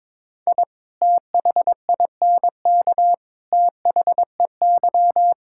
Morse code, heard as a keyed tone, about 22 wpm, 700 Hz. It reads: I THINK THEY
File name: practice / 22wpm / Sentences with 3-Word Phrases - 22wpm